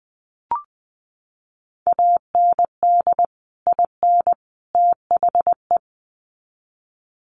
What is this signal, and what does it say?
Morse code, heard as a keyed tone, about 20 wpm, 700 Hz. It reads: AND IN THE
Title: Sets of 3 Words 20wpm